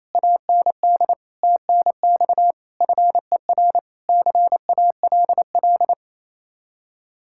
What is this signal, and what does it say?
Morse code, heard as a keyed tone, about 36 wpm, 700 Hz. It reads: AND TNX FER CALL